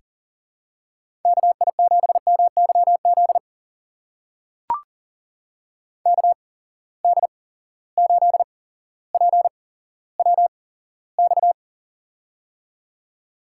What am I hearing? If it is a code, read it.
Morse code, heard as a keyed tone, about 39 wpm, spaced as for 17 wpm, 700 Hz. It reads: KI7MYZ KD8PWX
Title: USA Call Signs - 8x Character Spacing - 40wpm